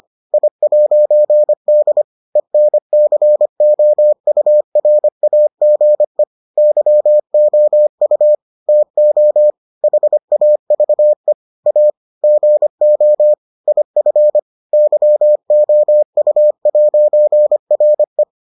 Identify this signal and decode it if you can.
Morse code, heard as a keyed tone, about 25 wpm, 600 Hz. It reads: I'D ENCOURAGE YOU TO HAVE A GO IF YOU'RE